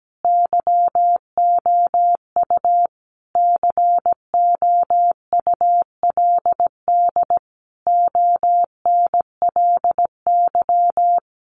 Morse code, heard as a keyed tone, about 17 wpm, 700 Hz. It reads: YOU COULD ONLY
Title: Sentences from Top 1100 Words 17wpm